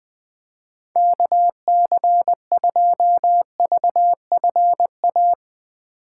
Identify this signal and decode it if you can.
Morse code, heard as a keyed tone, about 20 wpm, 700 Hz. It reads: KC2VFA
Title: USA Call Signs - 6 Characters - 2x Character Spacing - 20wpm